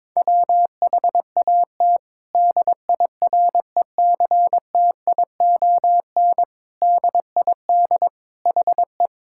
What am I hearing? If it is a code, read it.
Morse code, heard as a keyed tone, about 22 wpm, 700 Hz. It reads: WHAT DIRECTION DID HE
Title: Sentences from Top 1000 Words - Review Part 5 - 22wpm